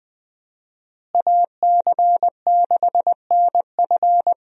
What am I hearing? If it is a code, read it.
Morse code, heard as a keyed tone, about 20 wpm, 700 Hz. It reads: AC6NF
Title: USA Call Signs - 5 Characters - Repeated 2x - 20wpm